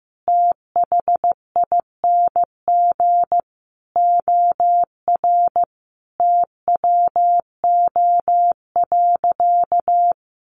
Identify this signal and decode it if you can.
Morse code, heard as a keyed tone, about 15 wpm, 700 Hz. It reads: THING OR TWO.
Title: Sentences from Top 300 Words 15wpm